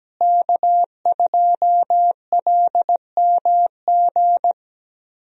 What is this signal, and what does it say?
Morse code, heard as a keyed tone, about 17 wpm, 700 Hz. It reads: K2LMG
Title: USA Call Signs - 5 Characters - Repeated 3x - 17wpm